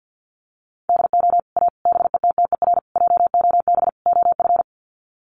Morse code, heard as a keyed tone, about 50 wpm, 700 Hz. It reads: DO A BETTER JOB OF
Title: Sets of 5 Words Encore2 50wpm